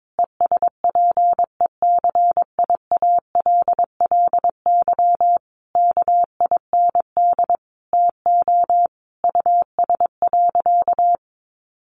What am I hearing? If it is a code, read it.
Morse code, heard as a keyed tone, about 22 wpm, 700 Hz. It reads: ESPECIALLY KIND TO US.
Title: Sentences from Top 700 Words 22wpm